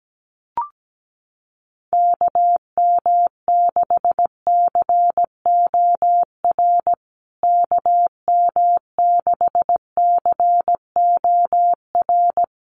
Morse code, heard as a keyed tone, about 17 wpm, 700 Hz. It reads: KM6COR KM6COR
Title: USA Call Signs Repeated 3x 17wpm